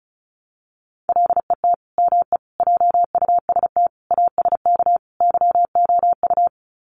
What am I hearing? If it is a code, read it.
Morse code, heard as a keyed tone, about 34 wpm, 700 Hz. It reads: LET ME JUST ASK YOU
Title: Sets of 5 Words Encore2 35wpm